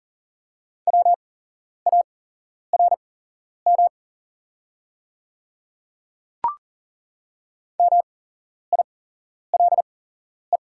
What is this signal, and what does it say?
Morse code, heard as a keyed tone, about 37 wpm, spaced as for 11 wpm, 700 Hz. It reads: WARM MILE
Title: 8x Character Spacing - Top 500 Words - 40wpm